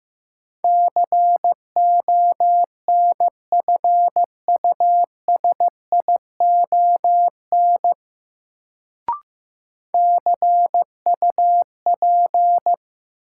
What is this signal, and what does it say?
Morse code, heard as a keyed tone, about 15 wpm, 700 Hz. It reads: CONFUSION CUP
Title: Sentences from Top 1600 Words - New Word Drill - 15wpm